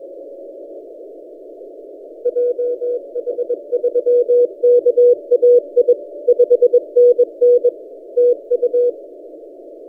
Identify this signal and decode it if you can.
Morse code, heard as a keyed tone, about 21 wpm, 490 Hz. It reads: JH3KAI 5NN TU
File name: posts / a35rt-10cw- -0134z